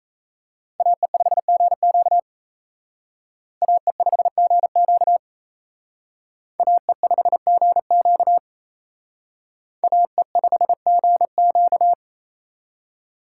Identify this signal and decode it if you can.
Morse code, heard as a keyed tone, about 35 wpm, 700 Hz. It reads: AE5GQ AE5GQ AE5GQ AE5GQ